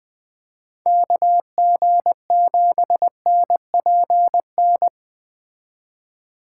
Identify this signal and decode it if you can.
Morse code, heard as a keyed tone, about 19 wpm, 700 Hz. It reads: KG7NPN
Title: Speed Racing - USA 2x3 Call Signs 17wpm